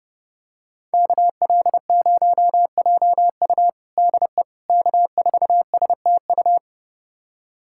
Morse code, heard as a keyed tone, about 29 wpm, 700 Hz. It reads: KL0JU DE K4STU
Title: USA Callsign DE Callsign - 5 Characters - 30wpm